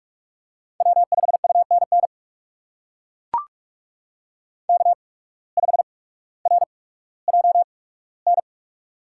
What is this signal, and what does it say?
Morse code, heard as a keyed tone, about 42 wpm, spaced as for 19 wpm, 700 Hz. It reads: W5UNN K5RJN
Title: USA Call Signs - 5 Characters - 8x Character Spacing - 45wpm